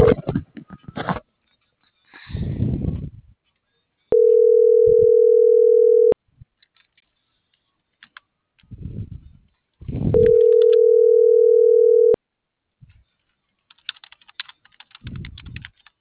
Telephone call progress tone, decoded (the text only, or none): ringback tone